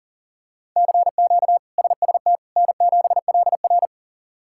Morse code, heard as a keyed tone, about 37 wpm, 700 Hz. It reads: CQ SST N7LR
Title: K1USN SST QSO Practice - 40wpm